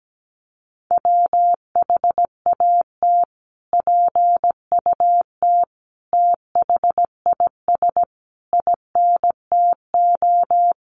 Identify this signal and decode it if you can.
Morse code, heard as a keyed tone, about 17 wpm, 700 Hz. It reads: WHAT PUT THIS INTO
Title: Sentences from Top 600 Words 17wpm